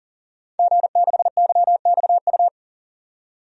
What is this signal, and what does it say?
Morse code, heard as a keyed tone, about 40 wpm, 700 Hz. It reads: G6YXU
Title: DXCC-100 Call Signs - 5 Characters - 40wpm